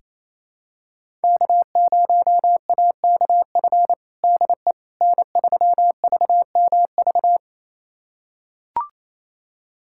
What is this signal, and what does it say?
Morse code, heard as a keyed tone, about 28 wpm, 700 Hz. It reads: K0AKF DE N3VMV
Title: USA Callsign DE Callsign - 5 Characters - 28wpm